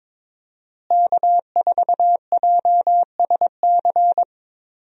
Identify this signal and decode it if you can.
Morse code, heard as a keyed tone, about 22 wpm, 700 Hz. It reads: K4JSC
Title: USA Call Signs - 5 Characters - Repeated 3x - 22wpm